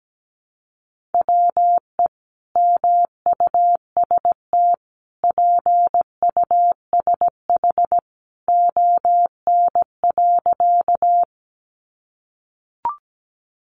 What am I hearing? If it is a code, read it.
Morse code, heard as a keyed tone, about 17 wpm, 700 Hz. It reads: WE MUST PUSH ON.
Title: Sentences from Top 1300 Words 17wpm